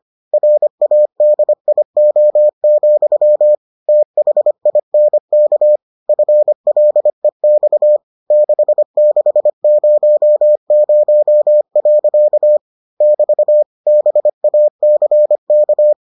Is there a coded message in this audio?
Yes: Morse code from 0.3 to 15.9 s, heard as a keyed tone, about 25 wpm, 600 Hz. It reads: RADIO, THINK FLEX 6600. <BT> BACK